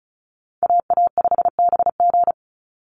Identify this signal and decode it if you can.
Morse code, heard as a keyed tone, about 34 wpm, 700 Hz. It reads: AA5BG
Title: Call Signs - 35wpm